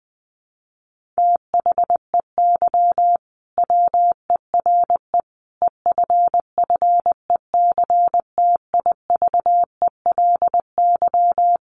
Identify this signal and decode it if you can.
Morse code, heard as a keyed tone, about 20 wpm, 700 Hz. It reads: THEY WERE EFFECTIVELY
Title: Sentences from Top 1400 Words 20wpm